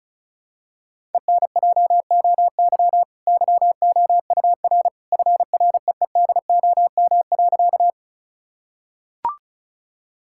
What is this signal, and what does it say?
Morse code, heard as a keyed tone, about 36 wpm, 700 Hz. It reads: ENJOY YOUR FREEDOM.
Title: Sentences from Top 1000 Words - Review Part 5 - 35wpm